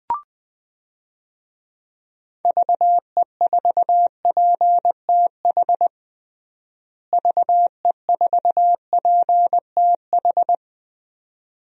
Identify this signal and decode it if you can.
Morse code, heard as a keyed tone, about 20 wpm, 700 Hz. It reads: VE4PTH VE4PTH
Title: DXCC-100 Call Signs - 6 Characters - Repeated 2x - 20wpm